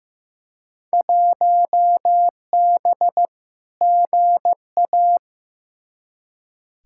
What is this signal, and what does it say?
Morse code, heard as a keyed tone, about 15 wpm, 700 Hz. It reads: 1B GA
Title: Field Day Exchanges Encore - 15wpm